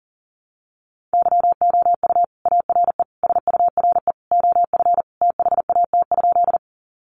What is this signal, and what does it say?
Morse code, heard as a keyed tone, about 38 wpm, 700 Hz. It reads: YOU ARE SURE OF THAT?
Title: Sentences from Top 300 Words 40wpm